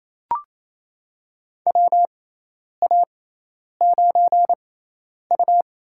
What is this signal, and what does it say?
Morse code, heard as a keyed tone, about 28 wpm, spaced as for 10 wpm, 700 Hz. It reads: WA9U
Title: USA Call Signs - 6x Character Spacing - 28wpm